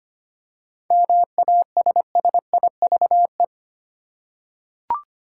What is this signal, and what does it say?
Morse code, heard as a keyed tone, about 25 wpm, 700 Hz. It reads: MASSIVE
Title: Seven Letter Words - 25wpm